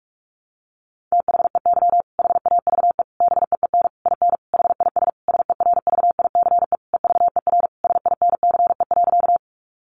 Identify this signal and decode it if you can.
Morse code, heard as a keyed tone, about 44 wpm, 700 Hz. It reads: THEY HAVE BEEN IN HIS SERVICE EVER SINCE.
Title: Sentences from Top 700 Words 45wpm